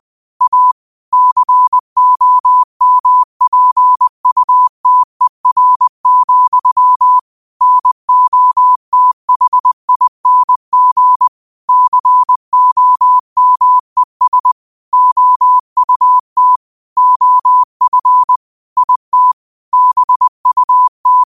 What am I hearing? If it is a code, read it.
Morse code, heard as a keyed tone, about 20 wpm, 1000 Hz. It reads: A COMPUTER, NOTHING COMES OUT OF IT BUT
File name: qotdfiles / 20wpm